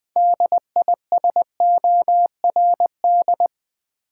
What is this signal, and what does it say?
Morse code, heard as a keyed tone, about 20 wpm, 700 Hz. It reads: DISORD
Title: Build A Word - 2 Letters At A Time - 8-Letter Words - 20wpm